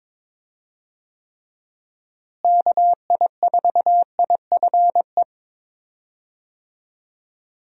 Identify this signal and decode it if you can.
Morse code, heard as a keyed tone, about 22 wpm, 700 Hz. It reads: KI4IFE
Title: USA Call Signs - 6 Characters - Repeated 3x - 22wpm